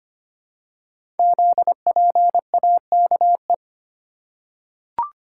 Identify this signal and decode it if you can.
Morse code, heard as a keyed tone, about 25 wpm, 700 Hz. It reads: ZPAKE
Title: Code Groups - 5 Letter:Numbers - 25wpm